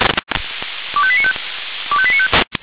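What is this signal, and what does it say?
Five-tone selective call (ZVEI), heard as quick runs of five tones, tones of about 70 ms. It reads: 25895 25895
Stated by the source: ZVEI five-tone selective call received on SDR.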